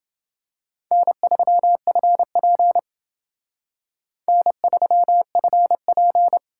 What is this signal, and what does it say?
Morse code, heard as a keyed tone, about 28 wpm, 700 Hz. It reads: N3FP N3FP